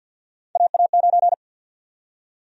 Morse code, heard as a keyed tone, about 52 wpm, 700 Hz. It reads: AA9
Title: USA Call Sign Prefixes - 50wpm